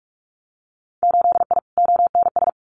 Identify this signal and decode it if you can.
Morse code, heard as a keyed tone, about 46 wpm, 700 Hz. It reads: 8I ONS
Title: Winter Field Day Exchanges Encore 45wpm